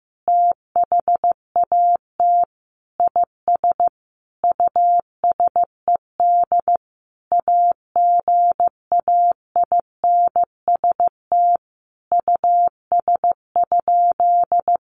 Morse code, heard as a keyed tone, about 15 wpm, 700 Hz. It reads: THAT IS USED AGAINST US?
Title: Sentences from Top 1500 Words 15wpm